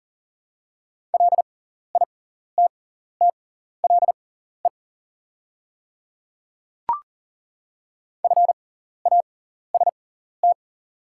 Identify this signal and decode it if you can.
Morse code, heard as a keyed tone, about 40 wpm, spaced as for 14 wpm, 690 Hz. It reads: LITTLE FAST
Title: 6x Character Spacing - Top 500 Words - 40wpm